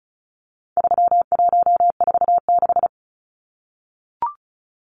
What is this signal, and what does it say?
Morse code, heard as a keyed tone, about 34 wpm, 700 Hz. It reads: 3146
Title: Numbers - 4-Digit 35wpm